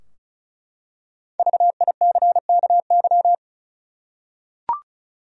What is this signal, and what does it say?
Morse code, heard as a keyed tone, about 36 wpm, 700 Hz. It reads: VICKY
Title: USA Common Names - Max 5 Letters - 35wpm